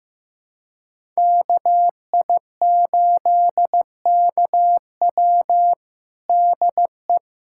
Morse code, heard as a keyed tone, about 15 wpm, 700 Hz. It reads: KI8KW DE
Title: USA Callsign DE Callsign - 5 Characters - 15wpm